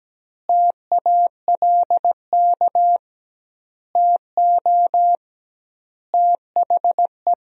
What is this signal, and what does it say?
Morse code, heard as a keyed tone, about 17 wpm, 700 Hz. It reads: TALK TO THE
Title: 2x Word-Spacing - Sentences from Top 400 Words - 17wpm